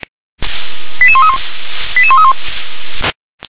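Five-tone selective call (ZVEI), heard as quick runs of five tones, tones of about 70 ms. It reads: 88131 88131
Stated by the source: ZVEI five-tone selective call 88911 received by SDR.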